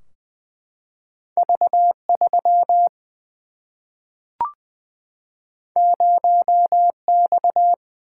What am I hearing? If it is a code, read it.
Morse code, heard as a keyed tone, about 20 wpm, 700 Hz. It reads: V3 0X
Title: Single-Letter Single-Number 20wpm